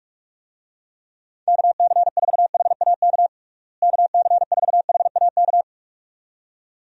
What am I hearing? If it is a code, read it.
Morse code, heard as a keyed tone, about 46 wpm, 700 Hz. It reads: KC4HAK KC4HAK
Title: USA Call Signs - 6 Characters - Repeated 2x - 45wpm